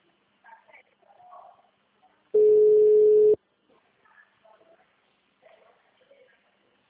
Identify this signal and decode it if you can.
Ringback tone (Europe, 425 Hz), with speech.
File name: out- -2005- - - .55971